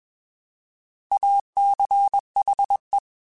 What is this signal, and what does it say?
Morse code, heard as a keyed tone, about 21 wpm, 800 Hz. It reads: ACHE
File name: Alphabetical / Alphabetical mp3s / As @20